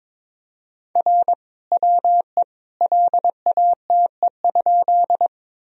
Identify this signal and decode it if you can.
Morse code, heard as a keyed tone, about 22 wpm, 700 Hz. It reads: R WE LATE?
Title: Sentences from Top 300 Words and Abbreviations 22wpm